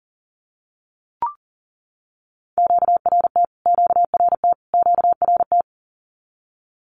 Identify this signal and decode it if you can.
Morse code, heard as a keyed tone, about 40 wpm, 700 Hz. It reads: QRT QRT QRT
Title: Q-Codes Repeated 3x 40wpm